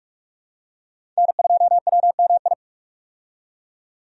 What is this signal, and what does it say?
Morse code, heard as a keyed tone, about 42 wpm, 700 Hz. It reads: N2WMI
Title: USA Call Signs - 5 Characters - 45wpm